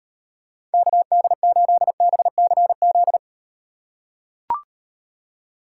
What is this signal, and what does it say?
Morse code, heard as a keyed tone, about 38 wpm, 700 Hz. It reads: KD8BCZ